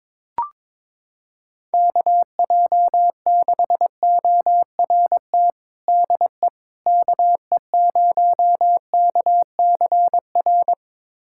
Morse code, heard as a keyed tone, about 22 wpm, 700 Hz. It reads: KJ6ORT DE KE0KCR
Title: USA Callsign DE Callsign - 6 Characters - 22wpm